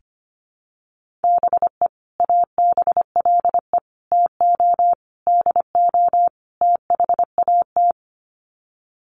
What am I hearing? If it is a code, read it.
Morse code, heard as a keyed tone, about 25 wpm, 700 Hz. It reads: BE ABLE TO DO THAT
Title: Sets of 5 Words Encore2 25wpm